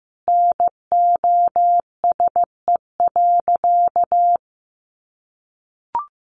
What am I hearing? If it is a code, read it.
Morse code, heard as a keyed tone, about 14 wpm, 700 Hz. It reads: NOSE.
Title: Easy Phrases 15wpm